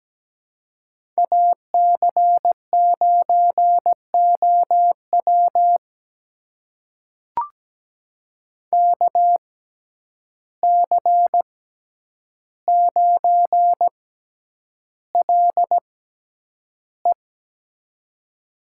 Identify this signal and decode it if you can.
Morse code, heard as a keyed tone, about 17 wpm, spaced as for 9 wpm, 700 Hz. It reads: AC9OW KC9LE
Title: USA Call Signs - 5 Characters - 6x Character Spacing - 17wpm